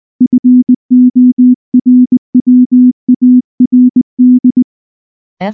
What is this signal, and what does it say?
Morse code, heard as a keyed tone, about 19 wpm, 250 Hz. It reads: FORWARD
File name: morse / morse-training-20WPM-5